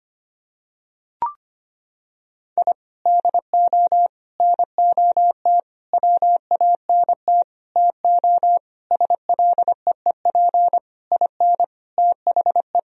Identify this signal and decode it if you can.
Morse code, heard as a keyed tone, about 25 wpm, 700 Hz. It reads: I DO NOT WANT TO SLEEP IN THE
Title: Sentences from Top 1000 Words - Review Part 2 - 25wpm